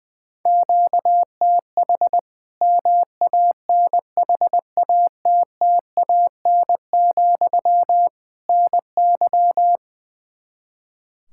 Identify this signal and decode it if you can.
Morse code, heard as a keyed tone, about 21 wpm, 700 Hz. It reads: QTH MANHATTAN, NY
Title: Practice QSO - Exchange 1 - Beginning - 20wpm